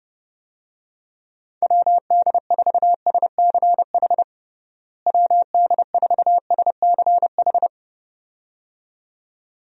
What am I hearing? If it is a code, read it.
Morse code, heard as a keyed tone, about 30 wpm, 700 Hz. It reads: WD4SCH WD4SCH
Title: USA Call Signs - 6 Characters - Repeated 2x - 30wpm